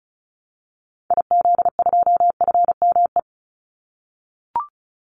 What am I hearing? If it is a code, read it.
Morse code, heard as a keyed tone, about 34 wpm, 700 Hz. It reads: IZ2FME
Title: Call Signs - 35wpm